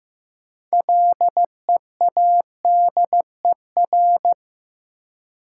Morse code, heard as a keyed tone, about 15 wpm, 700 Hz. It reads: LEADER
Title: Six Letter Words - 15wpm